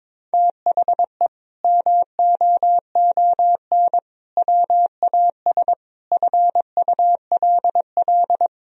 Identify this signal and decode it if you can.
Morse code, heard as a keyed tone, about 21 wpm, 700 Hz. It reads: THE MOON WAS FULL
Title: Sentences from Top 1000 Words - Review Part 3 - 22wpm